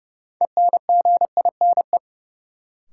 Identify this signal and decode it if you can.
Morse code, heard as a keyed tone, about 31 wpm, 700 Hz. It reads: ENGINE
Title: Sentences from Top 1500 Words - New Word Drill Alt - 30wpm